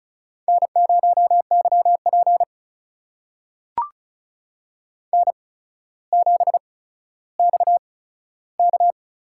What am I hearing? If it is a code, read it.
Morse code, heard as a keyed tone, about 35 wpm, spaced as for 15 wpm, 700 Hz. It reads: N0YP N7XK
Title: USA Call Signs - 4 Characters - 8x Character Spacing - 35wpm